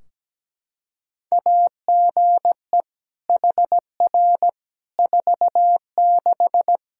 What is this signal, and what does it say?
Morse code, heard as a keyed tone, about 17 wpm, 700 Hz. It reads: AGE HR 46
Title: Practice QSO - Exchange 3 - More Info - 17wpm